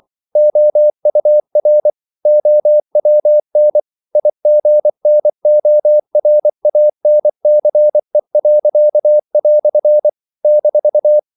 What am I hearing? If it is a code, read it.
Morse code, heard as a keyed tone, about 24 wpm, 600 Hz. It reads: OUR OWN IGNORANCE." -
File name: CW / Practice / EnglishQuotations0224